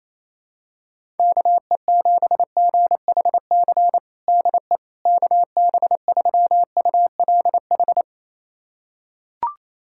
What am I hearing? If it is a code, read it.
Morse code, heard as a keyed tone, about 28 wpm, 700 Hz. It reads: KE7GHC DE KB3ULH
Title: USA Callsign DE Callsign - 6 Characters - 28wpm